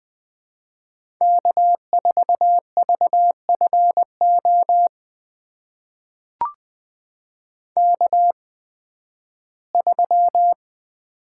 Morse code, heard as a keyed tone, about 21 wpm, spaced as for 13 wpm, 700 Hz. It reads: K4VFO K3
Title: USA Call Signs - 5 Characters - 8x Character Spacing - 20wpm